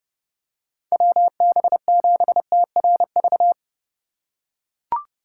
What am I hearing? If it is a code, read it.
Morse code, heard as a keyed tone, about 30 wpm, 700 Hz. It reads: WB7TRV
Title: USA Call Signs - 6 Characters - 4x Character Spacing - 30wpm